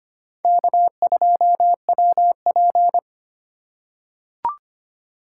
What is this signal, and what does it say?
Morse code, heard as a keyed tone, about 25 wpm, 700 Hz. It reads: K2WP
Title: USA Call Signs - 4 Characters - 4x Character Spacing - 25wpm